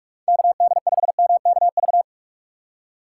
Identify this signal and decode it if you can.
Morse code, heard as a keyed tone, about 43 wpm, 700 Hz. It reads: KD5MKV
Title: USA Call Signs - 6 Characters - 3x Character Spacing - 45wpm